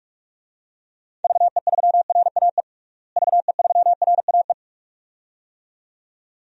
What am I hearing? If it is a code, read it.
Morse code, heard as a keyed tone, about 46 wpm, 700 Hz. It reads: VE3RAE VE3RAE
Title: DXCC-100 Call Signs - 6 Characters - Repeated 2x - 45wpm